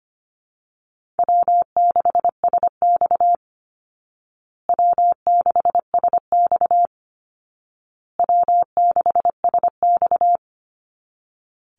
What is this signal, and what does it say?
Morse code, heard as a keyed tone, about 25 wpm, 700 Hz. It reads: W6SX W6SX W6SX
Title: Easy Call Signs Repeated 3x - 25wpm